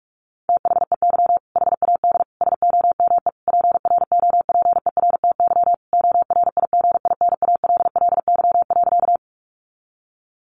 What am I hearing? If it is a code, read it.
Morse code, heard as a keyed tone, about 44 wpm, 700 Hz. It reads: THEY HAD SOME PROPERTY ORIGINALLY.
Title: Sentences from Top 1500 Words 45wpm